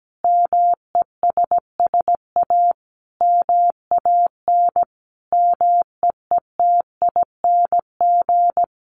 Morse code, heard as a keyed tone, about 17 wpm, 700 Hz. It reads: MESSA MAN MEETING
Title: A Sound Of Thunder 17wpm